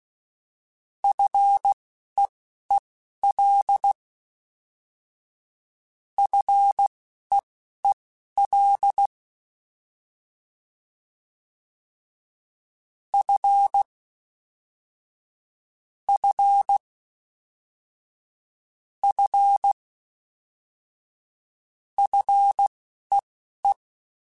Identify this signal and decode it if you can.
Morse code, heard as a keyed tone, about 16 wpm, spaced as for 11 wpm, 800 Hz. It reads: FEEL FEEL F F F FEE
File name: F and L / F and L mp3 / F and L Prefixes 3x2 Twice @10